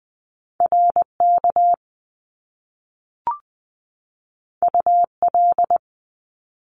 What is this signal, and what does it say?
Morse code, heard as a keyed tone, about 20 wpm, 700 Hz. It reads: RK UL
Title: Top Two Letter Combinations in Words 20wpm